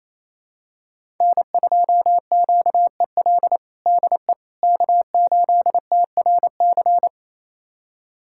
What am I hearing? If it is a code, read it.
Morse code, heard as a keyed tone, about 28 wpm, 700 Hz. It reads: N2QEL DE K8TRC
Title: USA Callsign DE Callsign - 5 Characters - 28wpm